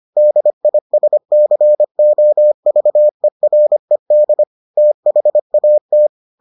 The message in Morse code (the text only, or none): DISCOVERED THAT